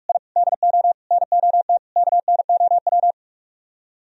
Morse code, heard as a keyed tone, about 45 wpm, 700 Hz. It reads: I DO NOT KNOW